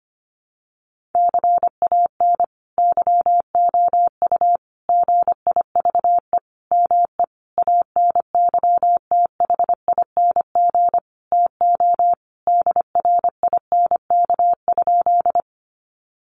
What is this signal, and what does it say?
Morse code, heard as a keyed tone, about 25 wpm, 700 Hz. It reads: CAN YOU GIVE ME ANYTHING TO DRINK?
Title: Sentences from Top 1500 Words 25wpm